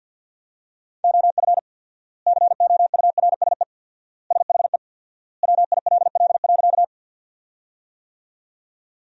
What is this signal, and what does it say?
Morse code, heard as a keyed tone, about 46 wpm, 700 Hz. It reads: OF COURSE SHE WILL.
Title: 4x Word-Spacing - Sentences from Top 500 Words - 50wpm